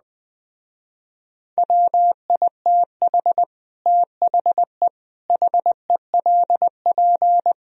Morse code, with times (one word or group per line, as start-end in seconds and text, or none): 1.6-3.4 WITH
3.9-4.9 THE
5.3-7.5 HELP